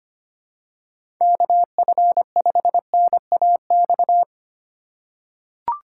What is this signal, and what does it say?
Morse code, heard as a keyed tone, about 25 wpm, 700 Hz. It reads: KF5NAX
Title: USA Call Signs - 4x Character Spacing - 25wpm